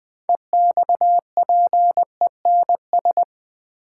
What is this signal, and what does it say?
Morse code, heard as a keyed tone, about 20 wpm, 700 Hz. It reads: EXPENS
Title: Build A Word - 3 Letters At A Time - 9-Letter Words - 20wpm